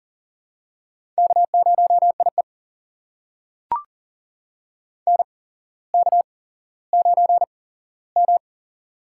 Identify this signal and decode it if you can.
Morse code, heard as a keyed tone, about 40 wpm, spaced as for 17 wpm, 700 Hz. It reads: K0IE NK9M
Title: USA Call Signs - 4 Characters - 8x Character Spacing - 40wpm